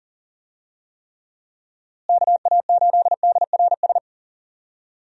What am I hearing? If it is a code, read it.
Morse code, heard as a keyed tone, about 40 wpm, 700 Hz. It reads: KA8DRS
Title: USA Call Signs - 6 Characters - Repeated 2x - 40wpm